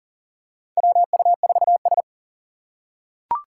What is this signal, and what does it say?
Morse code, heard as a keyed tone, about 40 wpm, 700 Hz. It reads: WU4S
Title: USA Call Signs - 4 Characters - 2x Character Spacing - 40wpm